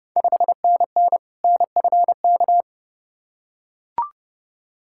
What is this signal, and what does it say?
Morse code, heard as a keyed tone, about 30 wpm, 700 Hz. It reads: 5NN NFK
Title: Ontario QSO Party Exchange 30wpm